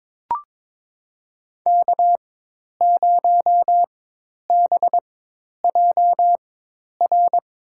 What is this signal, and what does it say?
Morse code, heard as a keyed tone, about 22 wpm, spaced as for 10 wpm, 700 Hz. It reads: K0BJR
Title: USA Call Signs - 4x Character Spacing - 22wpm